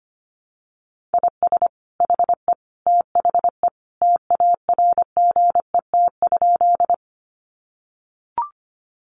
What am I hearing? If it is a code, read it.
Morse code, heard as a keyed tone, about 25 wpm, 700 Hz. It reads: IS HE THE TARGET?
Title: Sentences from Top 1200 Words 25wpm